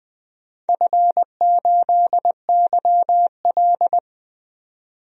Morse code, heard as a keyed tone, about 20 wpm, 700 Hz. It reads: F8YL
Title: DXCC-100 Call Signs - 4 Characters - 20wpm